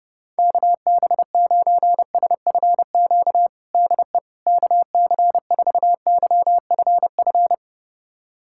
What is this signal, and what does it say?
Morse code, heard as a keyed tone, about 32 wpm, 700 Hz. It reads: KB9SFQ DE KC4YFF
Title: USA Callsign DE Callsign - 6 Characters - 30wpm